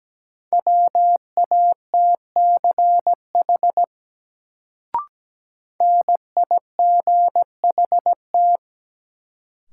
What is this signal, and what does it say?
Morse code, heard as a keyed tone, about 17 wpm, 700 Hz. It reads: WATCH NIGHT
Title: Sentences from Top 300 Words - New Word Drill - 17wpm